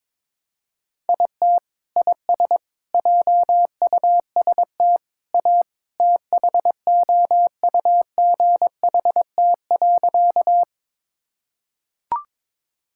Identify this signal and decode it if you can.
Morse code, heard as a keyed tone, about 22 wpm, 700 Hz. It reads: IT IS JUST A THOUGHT.
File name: practice / 22wpm / Sentences with 2-Word Phrases - 22wpm